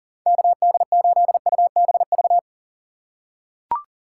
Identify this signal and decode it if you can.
Morse code, heard as a keyed tone, about 40 wpm, 700 Hz. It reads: KD8UBV
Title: USA Call Signs - 6 Characters - 4x Character Spacing - 40wpm